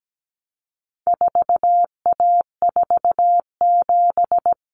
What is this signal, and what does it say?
Morse code, heard as a keyed tone, about 17 wpm, 700 Hz. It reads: 4A47
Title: Common Cut Numbers - 4-Digit 17wpm